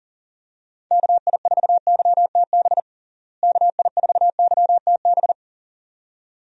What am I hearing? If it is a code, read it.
Morse code, heard as a keyed tone, about 40 wpm, 700 Hz. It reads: KI4YTB KI4YTB
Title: USA Call Signs - 6 Characters - Repeated 2x - 40wpm